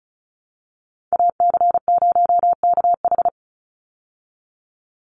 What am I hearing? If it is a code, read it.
Morse code, heard as a keyed tone, about 34 wpm, 700 Hz. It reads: AC0KH
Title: USA Call Signs 35wpm